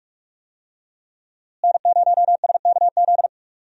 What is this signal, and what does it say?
Morse code, heard as a keyed tone, about 43 wpm, 700 Hz. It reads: N0SKZ
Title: USA Call Signs - 5 Characters - Repeated 3x - 45wpm